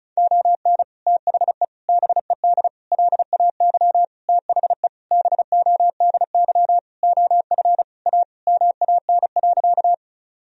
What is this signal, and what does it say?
Morse code, heard as a keyed tone, about 36 wpm, 700 Hz. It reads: ON THE BED LAY THE BODY OF A MAN.
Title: Sentences from Top 1000 Words - Review Part 2 - 35wpm